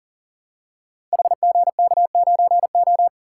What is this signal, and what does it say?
Morse code, heard as a keyed tone, about 40 wpm, 690 Hz. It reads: HGK9O
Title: Code Groups - 5 Letter:Numbers - 40wpm